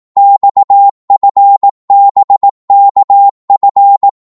Morse code, heard as a keyed tone, about 18 wpm, 800 Hz. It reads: XFBKF
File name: Morse / ltr-18wpm